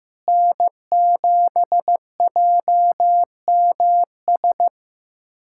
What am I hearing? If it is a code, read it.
Morse code, heard as a keyed tone, about 15 wpm, 700 Hz. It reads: N7JMS
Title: USA Callsign DE Callsign - 5 Characters - 15wpm